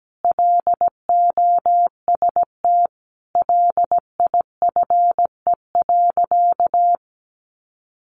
Morse code, heard as a keyed tone, about 17 wpm, 700 Hz. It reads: LOST LIFE.
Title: Sentences from Top 1400 Words 17wpm